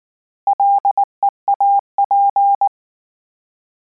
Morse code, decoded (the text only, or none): LEAP